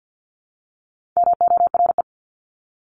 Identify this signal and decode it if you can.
Morse code, heard as a keyed tone, about 50 wpm, 700 Hz. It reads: MORE
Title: Top 500 Words 50wpm